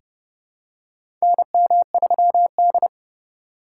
Morse code, heard as a keyed tone, about 30 wpm, 700 Hz. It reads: NM3D
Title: USA Call Signs - 4 Characters - 2x Character Spacing - 30wpm